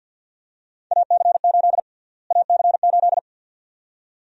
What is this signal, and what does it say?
Morse code, heard as a keyed tone, about 51 wpm, 700 Hz. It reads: AC8 AC8
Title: USA Call Sign Prefixes - Repeated 2x - 50wpm